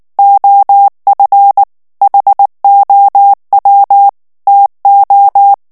Morse code, heard as a keyed tone, about 19 wpm, 800 Hz. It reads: OF HOW TO
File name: morse mp3 / ANGLAIS texte clair 2 vitesse 18 mots minute